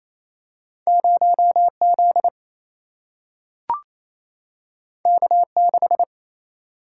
Morse code, heard as a keyed tone, about 28 wpm, 700 Hz. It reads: 0 Z K 6
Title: Single-Letter Single-Number 28wpm